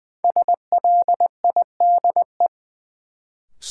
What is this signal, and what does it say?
Morse code, heard as a keyed tone, about 20 wpm, 700 Hz. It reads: SLIDE
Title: Sentences from Top 1600 Words - New Word Drill - 20wpm